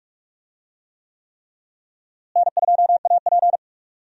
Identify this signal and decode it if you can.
Morse code, heard as a keyed tone, about 42 wpm, 700 Hz. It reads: N2AP
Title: USA Call Signs - 4 Characters - Repeated 3x - 45wpm